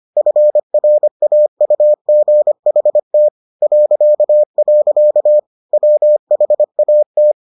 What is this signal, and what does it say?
Morse code, heard as a keyed tone, about 25 wpm, 600 Hz. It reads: FRAUGHT .. WHAT